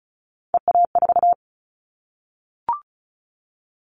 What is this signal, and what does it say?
Morse code, heard as a keyed tone, about 36 wpm, 700 Hz. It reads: EA4
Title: Callsign Prefixes for Top 25 DXCC - 35wpm